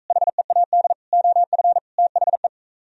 Morse code, heard as a keyed tone, about 45 wpm, 700 Hz. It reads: HEAD OF THE